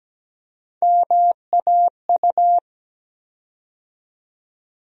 Morse code, heard as a keyed tone, about 17 wpm, 700 Hz. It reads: MAU
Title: USA Call Sign Postfixes - 3 Letters 17wpm